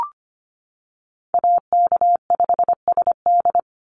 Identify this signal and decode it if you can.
Morse code, heard as a keyed tone, about 25 wpm, 700 Hz. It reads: AK5SD
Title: NA SOTA Chaser Callsigns 25wpm